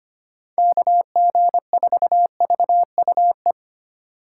Morse code, heard as a keyed tone, about 24 wpm, 700 Hz. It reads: KG4VUE
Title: Speed Racing - USA 2x3 Call Signs 25wpm